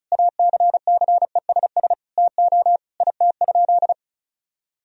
Morse code, heard as a keyed tone, about 36 wpm, 700 Hz. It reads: ACCESS TO IT?
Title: Sentences from Top 1000 Words - Review Part 4 - 35wpm